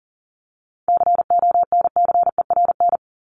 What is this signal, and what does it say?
Morse code, heard as a keyed tone, about 40 wpm, 700 Hz. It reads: CONCERN
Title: ING Suffix - Top 100 Words with Base Words 40wpm